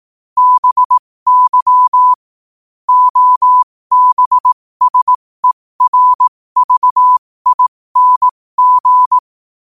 Morse code, heard as a keyed tone, about 18 wpm, spaced as for 15 wpm, 1000 Hz. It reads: BY OBSERVING
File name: qotdfiles / 15wpm